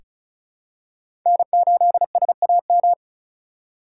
Morse code, heard as a keyed tone, about 36 wpm, 700 Hz. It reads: N8SAM
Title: USA Call Signs - 5 Characters - 4x Character Spacing - 35wpm